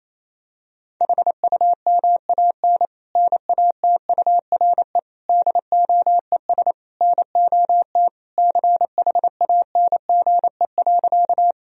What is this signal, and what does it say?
Morse code, heard as a keyed tone, about 28 wpm, 700 Hz. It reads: HUMAN NATURE DOES NOT CHANGE.
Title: Sentences from Top 1000 Words - Review Part 3 - 28wpm